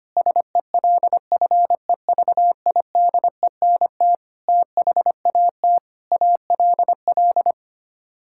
Morse code, heard as a keyed tone, about 25 wpm, 700 Hz. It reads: SELFEVIDENT THAT ALL
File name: Sets of 9 Words - 25wpm